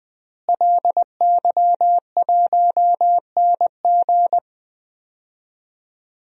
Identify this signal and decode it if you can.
Morse code, heard as a keyed tone, about 20 wpm, 700 Hz. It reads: LY1NG
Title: DXCC-100 Call Signs - 5 Characters - Repeated 3x - 20wpm